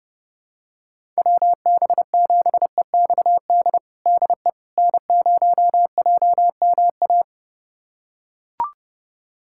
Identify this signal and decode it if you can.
Morse code, heard as a keyed tone, about 32 wpm, 700 Hz. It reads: WB7EXD DE N0JMA
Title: USA Callsign DE Callsign 30wpm